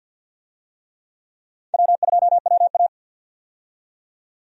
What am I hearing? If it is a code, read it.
Morse code, heard as a keyed tone, about 49 wpm, 700 Hz. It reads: W2WA
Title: USA Call Signs - 4 Characters - 50wpm